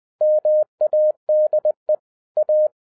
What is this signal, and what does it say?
Morse code, heard as a keyed tone, about 20 wpm, 610 Hz. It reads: MADE A
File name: 2012-01-mp3 / n15-INVESTIGATION INTO THE POSSIBILITY OF WINNING PREFIX VERSION 3 AWARD-20wpm